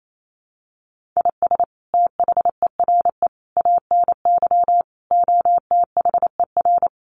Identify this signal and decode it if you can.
Morse code, heard as a keyed tone, about 28 wpm, 700 Hz. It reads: IS THERE ANY OTHER
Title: Sentences from Top 1200 Words 28wpm